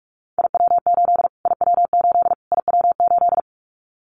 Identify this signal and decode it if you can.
Morse code, heard as a keyed tone, about 44 wpm, 700 Hz. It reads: IW8 IW8 IW8
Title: Callsign Prefixes for Top 25 DXCC - Repeated 3x - 45wpm